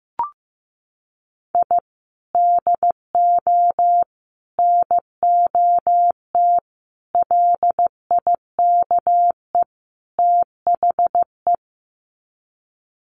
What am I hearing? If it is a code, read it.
Morse code, heard as a keyed tone, about 14 wpm, 700 Hz. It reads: I DO NOT LIKE THE
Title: Sets of 5 Words Encore 15wpm